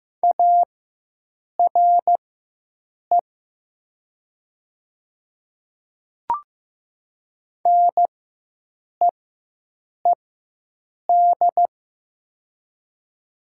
Morse code, heard as a keyed tone, about 15 wpm, spaced as for 7 wpm, 700 Hz. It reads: ARE NEED
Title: 4x Character Spacing - Top 200 Words - 15wpm